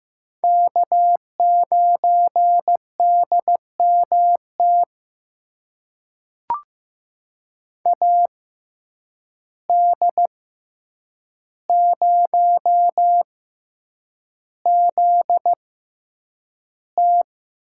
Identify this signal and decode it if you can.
Morse code, heard as a keyed tone, about 15 wpm, spaced as for 8 wpm, 700 Hz. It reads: K9DMT AD0ZT
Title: USA Call Signs - 5 Characters - 6x Character Spacing - 15wpm